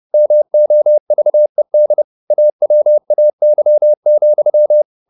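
Morse code, heard as a keyed tone, about 30 wpm, 600 Hz. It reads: MOVED AWAY,
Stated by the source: Morse code at 30 wpm, the novel Never by Ken Follett in English, from CW med Gnister.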